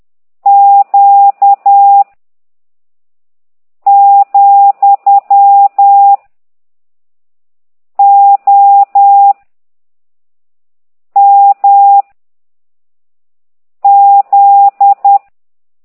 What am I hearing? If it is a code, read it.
Morse code, heard as a keyed tone, about 10 wpm, 800 Hz. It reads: Q , O M Z
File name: ct2idw / exerc11